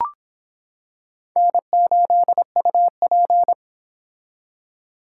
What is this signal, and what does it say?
Morse code, heard as a keyed tone, about 26 wpm, 700 Hz. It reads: N8UP